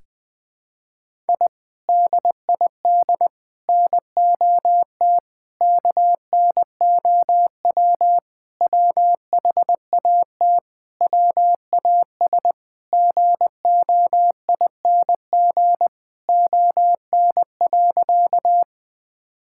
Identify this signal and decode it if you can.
Morse code, heard as a keyed tone, about 20 wpm, 700 Hz. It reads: I DID NOT KNOW WHAT WAS GOING ON.
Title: Sentences with 4-Word Phrases - 20wpm